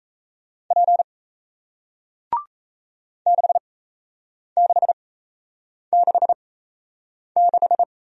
Morse code, heard as a keyed tone, about 37 wpm, spaced as for 8 wpm, 700 Hz. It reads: P 6666